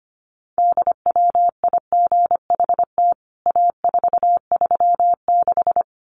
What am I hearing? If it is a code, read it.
Morse code, heard as a keyed tone, about 25 wpm, 700 Hz. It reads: DWIGHT A436
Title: CWT Exchange 25wpm